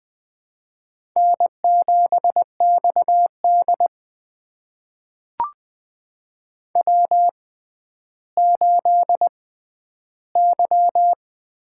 Morse code, heard as a keyed tone, about 20 wpm, spaced as for 11 wpm, 700 Hz. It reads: N7XD W8Y
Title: USA Call Signs - 4 Characters - 6x Character Spacing - 20wpm